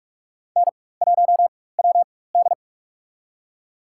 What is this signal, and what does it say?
Morse code, heard as a keyed tone, about 41 wpm, spaced as for 20 wpm, 700 Hz. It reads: N1WD
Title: USA Call Signs - 4x Character Spacing - 45wpm